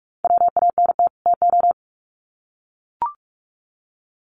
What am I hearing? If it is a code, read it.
Morse code, heard as a keyed tone, about 44 wpm, 700 Hz. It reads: WANT TO
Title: Sets of 2 Words 45wpm